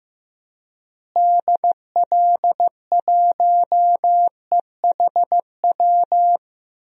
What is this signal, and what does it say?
Morse code, heard as a keyed tone, about 15 wpm, 700 Hz. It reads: DL1EHW
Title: DXCC-100 Call Signs - 6 Characters - Repeated 2x - 15wpm